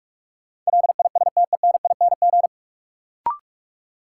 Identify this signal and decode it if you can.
Morse code, heard as a keyed tone, about 46 wpm, 700 Hz. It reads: LISTENING
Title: Sets of 5 Words - New Word Drill - 45wpm